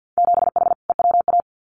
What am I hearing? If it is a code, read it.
Morse code, heard as a keyed tone, about 50 wpm, 700 Hz. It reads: 7H EWA
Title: Winter Field Day Exchanges Encore 50wpm